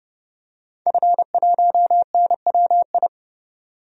Morse code, heard as a keyed tone, about 30 wpm, 700 Hz. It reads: F1NWI
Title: DXCC-100 Call Signs - 5 Characters - 30wpm